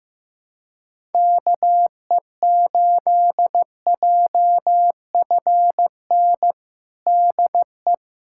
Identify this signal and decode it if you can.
Morse code, heard as a keyed tone, about 15 wpm, 700 Hz. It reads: KE8JFN DE
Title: USA Callsign DE Callsign - 6 Characters - 15wpm